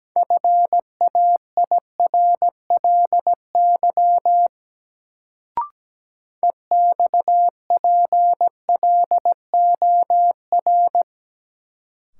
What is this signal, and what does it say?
Morse code, heard as a keyed tone, about 17 wpm, 700 Hz. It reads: FAIRLY EXPLOR
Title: Sentences from Top 1100 Words - New Word Drill Alt - 17wpm